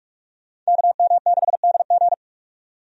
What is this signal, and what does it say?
Morse code, heard as a keyed tone, about 43 wpm, 700 Hz. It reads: KM6DG
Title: USA Call Signs - 5 Characters - 45wpm